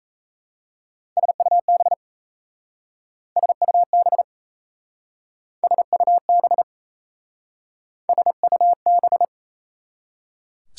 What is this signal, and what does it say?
Morse code, heard as a keyed tone, about 35 wpm, 700 Hz. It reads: SUB SUB SUB SUB